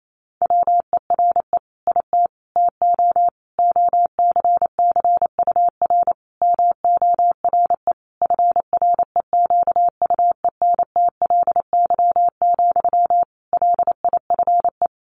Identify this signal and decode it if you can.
Morse code, heard as a keyed tone, about 28 wpm, 700 Hz. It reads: WERE IT TO OCCUR MORE FREQUENTLY, LIFE
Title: Sentences from Top 1300 Words 28wpm